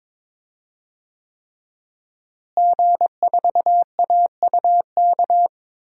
Morse code, heard as a keyed tone, about 22 wpm, 700 Hz. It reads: G4AUK
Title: DXCC-100 Call Signs - 5 Characters - Repeated 3x - 22wpm